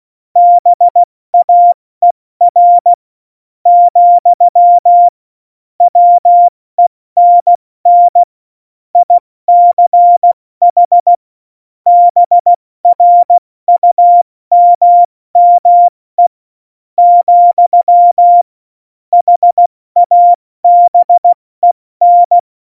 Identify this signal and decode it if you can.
Morse code, heard as a keyed tone, about 16 wpm, spaced as for 13 wpm, 700 Hz. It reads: BAER, WENN ICH BRUMME, HABEN
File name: Morsekurs / Witze / Witz 18. Dezember, 70 F80